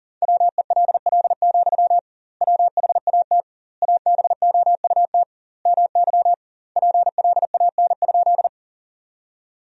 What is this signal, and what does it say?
Morse code, heard as a keyed tone, about 40 wpm, 700 Hz. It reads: WELL, WHAT ABOUT MY PLAN?
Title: 2x Word-Spacing - Sentences from Top 500 Words - 40wpm